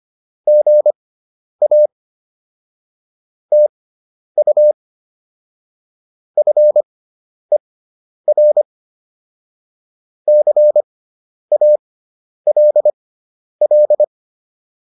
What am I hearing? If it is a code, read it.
Morse code, heard as a keyed tone, about 25 wpm, spaced as for 10 wpm, 600 Hz. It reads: GA TU FER CALL
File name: QSO104 10